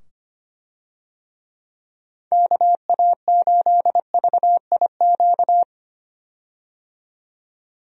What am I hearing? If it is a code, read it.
Morse code, heard as a keyed tone, about 25 wpm, 700 Hz. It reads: KA8VIQ
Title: USA Call Signs - 6 Characters - 25wpm